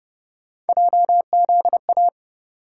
Morse code, heard as a keyed tone, about 29 wpm, 700 Hz. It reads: JZA